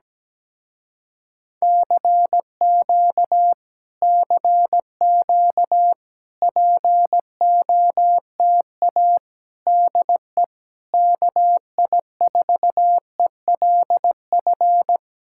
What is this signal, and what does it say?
Morse code, heard as a keyed tone, about 17 wpm, 700 Hz. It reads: CQ CQ POTA DE KI4ELF
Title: Speed-Racing Common QSO Phrases - 17wpm